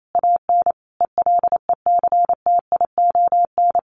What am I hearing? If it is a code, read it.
Morse code, heard as a keyed tone, about 28 wpm, 700 Hz. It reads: AN ELECTION
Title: Sentences from Top 1500 Words 28wpm